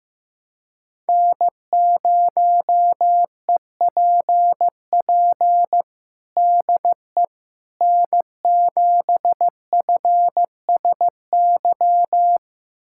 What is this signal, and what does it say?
Morse code, heard as a keyed tone, about 16 wpm, 700 Hz. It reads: N0EPP DE N7FSY
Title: USA Callsign DE Callsign - 5 Characters - 15wpm